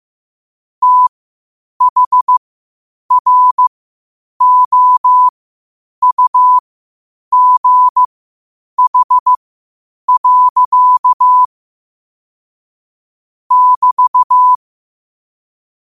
Morse code, heard as a keyed tone, about 15 wpm, spaced as for 8 wpm, 1000 Hz. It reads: THROUGH. <BT>
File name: qotdfiles / 8wpm